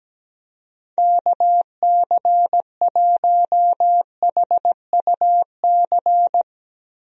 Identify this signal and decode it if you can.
Morse code, heard as a keyed tone, about 17 wpm, 700 Hz. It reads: KC1HUC
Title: USA Call Signs - 6 Characters - Repeated 2x - 17wpm